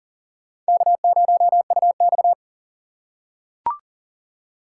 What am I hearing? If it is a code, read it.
Morse code, heard as a keyed tone, about 38 wpm, 700 Hz. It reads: K0UX
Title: USA Call Signs - 4 Characters - 3x Character Spacing - 40wpm